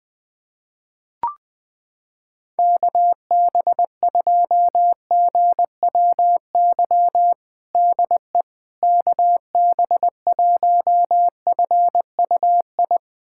Morse code, heard as a keyed tone, about 20 wpm, 700 Hz. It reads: KB2GWY DE KB1FUI
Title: USA Callsign DE Callsign - 6 Characters - 20wpm